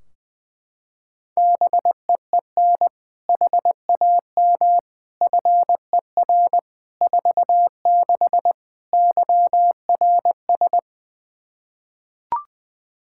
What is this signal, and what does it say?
Morse code, heard as a keyed tone, about 20 wpm, 700 Hz. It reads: BEEN HAM FER 46 YRS
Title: K1USN SST QSO Practice 20wpm